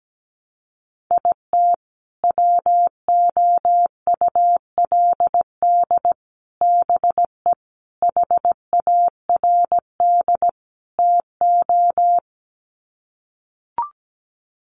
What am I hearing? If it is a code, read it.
Morse code, heard as a keyed tone, about 17 wpm, 700 Hz. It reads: IT WOULD BE HARD TO
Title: Sets of 5 Words Encore2 17wpm